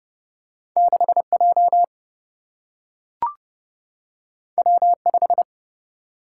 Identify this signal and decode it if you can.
Morse code, heard as a keyed tone, about 30 wpm, 700 Hz. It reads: 6J W5
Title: Single-Letter Single-Number 30wpm